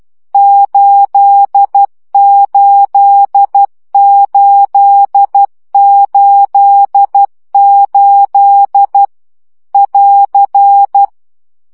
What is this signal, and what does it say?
Morse code, heard as a keyed tone, about 12 wpm, 800 Hz. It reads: 88888 <AR>
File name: CW gyakorlas 8 10 baud